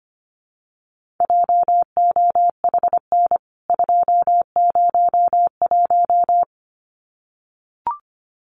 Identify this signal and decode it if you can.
Morse code, heard as a keyed tone, about 25 wpm, 700 Hz. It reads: JOHN 201
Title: CWT Exchange 25wpm